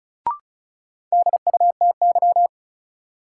Morse code, heard as a keyed tone, about 36 wpm, 700 Hz. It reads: DUTY
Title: Sentences from Top 1500 Words - New Word Drill Alt - 35wpm